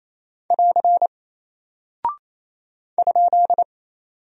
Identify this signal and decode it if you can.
Morse code, heard as a keyed tone, about 28 wpm, 700 Hz. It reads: <AR> ?
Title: Single Character (Letter:Number:Punctuation:Prosign) Alt 28wpm